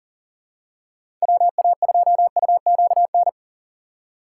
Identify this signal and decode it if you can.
Morse code, heard as a keyed tone, about 37 wpm, 700 Hz. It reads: WA2UQN
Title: USA Call Signs - 6 Characters - 2x Character Spacing - 40wpm